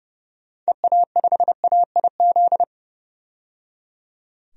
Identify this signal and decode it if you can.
Morse code, heard as a keyed tone, about 29 wpm, 690 Hz. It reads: EA5AIZ
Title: DXCC-100 Call Signs - 6 Characters - 30wpm